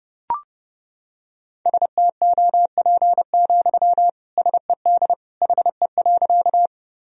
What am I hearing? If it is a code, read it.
Morse code, heard as a keyed tone, about 32 wpm, 700 Hz. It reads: STOP, SED HE.
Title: Sentences from Top 300 Words and Abbreviations 30wpm